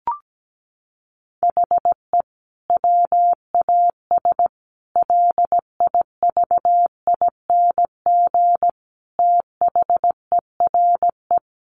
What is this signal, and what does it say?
Morse code, heard as a keyed tone, about 17 wpm, 700 Hz. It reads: HE WAS LIVING THERE
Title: Sentences from Top 1100 Words 17wpm